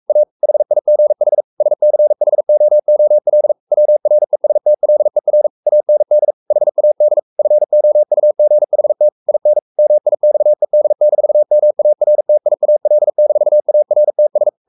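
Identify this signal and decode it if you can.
Morse code, heard as a keyed tone, about 44 wpm, 600 Hz. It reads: A HIGH SCHOOL WRESTLER AND HAD FOUGHT IN MIXED-MARTIAL-ARTS
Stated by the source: Morse code at 43 wpm sending Ken Follett's novel Never in English, from CW med Gnister.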